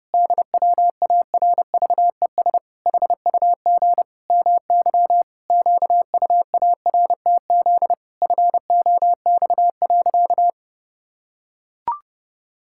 Morse code, heard as a keyed tone, about 30 wpm, 700 Hz. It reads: DWARVES HUG MY QUARTZ FOX.
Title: Pangram Challenge v2 - 30wpm